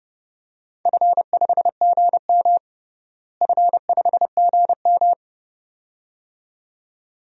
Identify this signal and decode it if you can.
Morse code, heard as a keyed tone, about 30 wpm, 700 Hz. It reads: F5GM F5GM
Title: DXCC-100 Call Signs - 4 Characters - Repeated 2x - 30wpm